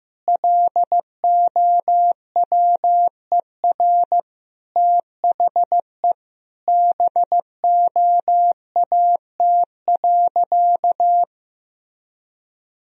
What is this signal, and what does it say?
Morse code, heard as a keyed tone, about 15 wpm, 700 Hz. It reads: LOWER THE BOAT.
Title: Sentences from Top 1000 Words - Review Part 4 - 15wpm